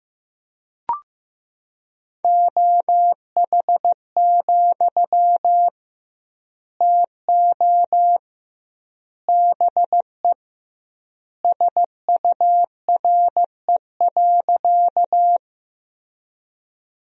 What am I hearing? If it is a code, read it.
Morse code, heard as a keyed tone, about 16 wpm, 700 Hz. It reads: OH, TO BE SURE.
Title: 2x Word-Spacing - Sentences from Top 500 Words - 15wpm